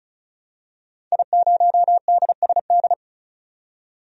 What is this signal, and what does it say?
Morse code, heard as a keyed tone, about 36 wpm, 700 Hz. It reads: I0DSD
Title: DXCC-100 Call Signs - 5 Characters - 35wpm